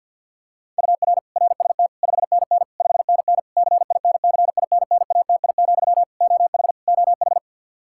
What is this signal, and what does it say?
Morse code, heard as a keyed tone, about 53 wpm, 700 Hz. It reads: UR RST 5NN 5NN CINCINNATI, OH OH
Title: Practice QSO - Exchange 1 - Beginning - 50wpm